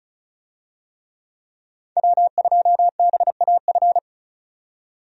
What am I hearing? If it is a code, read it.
Morse code, heard as a keyed tone, about 36 wpm, 700 Hz. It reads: W2BAF
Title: USA Call Signs - 5 Characters - 35wpm